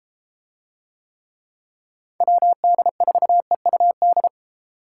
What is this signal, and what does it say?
Morse code, heard as a keyed tone, about 32 wpm, 700 Hz. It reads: WD4EUD